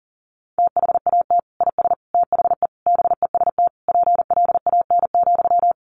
Morse code, heard as a keyed tone, about 40 wpm, 700 Hz. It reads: THAT IS THE BEST PLAN,
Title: Sentences from Top 1000 Words 40wpm